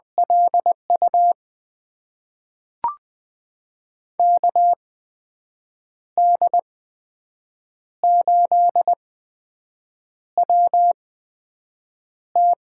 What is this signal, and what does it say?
Morse code, heard as a keyed tone, about 20 wpm, spaced as for 6 wpm, 700 Hz. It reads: LU KD8WT
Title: USA Call Signs - 6 Characters - 8x Character Spacing - 20wpm